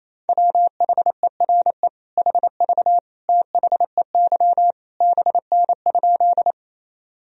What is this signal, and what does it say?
Morse code, heard as a keyed tone, about 28 wpm, 700 Hz. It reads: WHERE HV THEY BN?
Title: Sentences from Top 200 Words and Abbreviations 28wpm